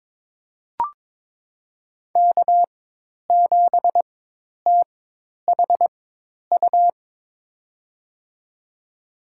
Morse code, heard as a keyed tone, about 22 wpm, spaced as for 10 wpm, 700 Hz. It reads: K7THU
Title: USA Call Signs - 4x Character Spacing - 22wpm